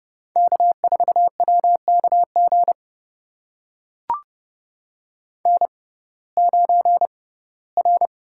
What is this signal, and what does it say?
Morse code, heard as a keyed tone, about 30 wpm, spaced as for 18 wpm, 700 Hz. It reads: K4WKG N9R
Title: USA Call Signs - 5 Characters - 6x Character Spacing - 30wpm